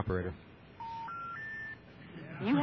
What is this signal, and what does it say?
Special information tone, with speech.